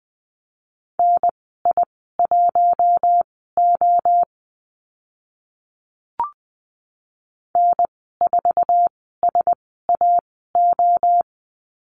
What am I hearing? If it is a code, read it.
Morse code, heard as a keyed tone, about 20 wpm, spaced as for 15 wpm, 700 Hz. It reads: NI1O N4SAO
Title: 088 Lesson 15g - Number 5 - Call Signs 20wpm